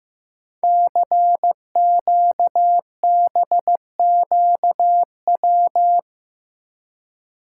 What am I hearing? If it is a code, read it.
Morse code, heard as a keyed tone, about 15 wpm, 700 Hz. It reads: CQBQW
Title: Code Groups - 5 Letter:Numbers - 15wpm